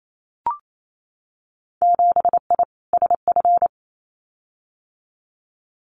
Morse code, heard as a keyed tone, about 28 wpm, 700 Hz. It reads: 7I SF
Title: Winter Field Day Exchanges Encore 28wpm